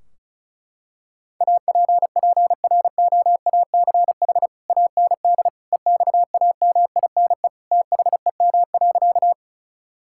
Morse code, heard as a keyed tone, about 36 wpm, 700 Hz. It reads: APPROACH AND EXAMINE THEM.
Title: Sentences from Top 1600 Words 35wpm